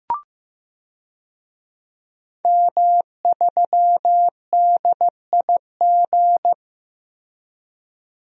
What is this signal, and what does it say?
Morse code, heard as a keyed tone, about 15 wpm, 700 Hz. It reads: M3DIG
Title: DXCC-100 Call Signs - 5 Characters - Repeated 3x - 15wpm